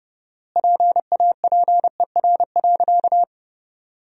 Morse code, heard as a keyed tone, about 30 wpm, 700 Hz. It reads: PAPER.
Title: 4x Word-Spacing - Sentences from Top 500 Words - 30wpm